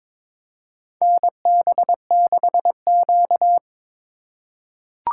Morse code, heard as a keyed tone, about 22 wpm, 700 Hz. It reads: NB6Q
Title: USA Call Signs - 4 Characters - 3x Character Spacing - 22wpm